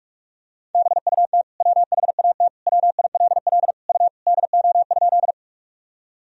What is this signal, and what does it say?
Morse code, heard as a keyed tone, about 43 wpm, 700 Hz. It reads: BUT WHAT WILL U DO?
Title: Sentences from Top 100 Words and Abbreviations 45wpm